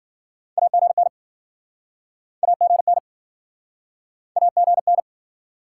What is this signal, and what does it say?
Morse code, heard as a keyed tone, about 56 wpm, 700 Hz. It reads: AGN AGN AGN